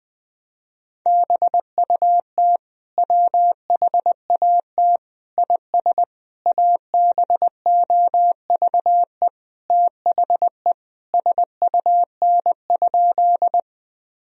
Morse code, heard as a keyed tone, about 20 wpm, 700 Hz. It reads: BUT WHAT IS ABOVE THE SUN?
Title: Sentences from Top 1000 Words - Review Part 2 - 20wpm